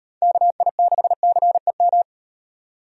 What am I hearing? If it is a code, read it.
Morse code, heard as a keyed tone, about 37 wpm, 700 Hz. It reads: KI6CEM